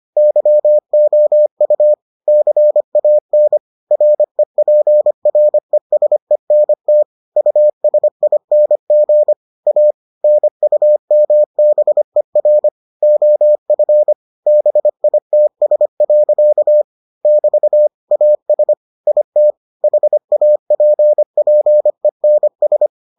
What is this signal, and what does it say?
Morse code, heard as a keyed tone, about 25 wpm, 600 Hz. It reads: YOU CAN REPRESENT USING A NUMBER OF BITS. <BT> AS IT HAPPENS